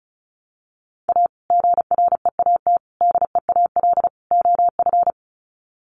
Morse code, heard as a keyed tone, about 34 wpm, 700 Hz. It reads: A GREAT DEAL OF
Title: Sets of 4 Words 35wpm